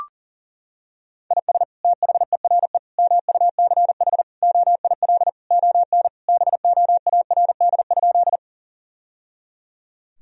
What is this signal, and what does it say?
Morse code, heard as a keyed tone, about 37 wpm, 700 Hz. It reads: IS THERE MUCH OIL ON BOARD?
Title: Sentences from Top 1000 Words - Review Part 4 - 40wpm